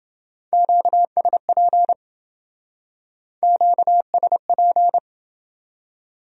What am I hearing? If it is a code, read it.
Morse code, heard as a keyed tone, about 28 wpm, 700 Hz. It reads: QSP QSP